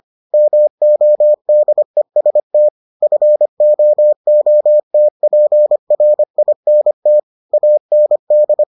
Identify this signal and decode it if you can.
Morse code, heard as a keyed tone, about 25 wpm, 600 Hz. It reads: MODEST FOOTPRINT AND